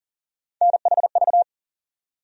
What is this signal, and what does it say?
Morse code, heard as a keyed tone, about 40 wpm, 700 Hz. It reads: NHV
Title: USA Call Sign Postfixes - 3 Letters 40wpm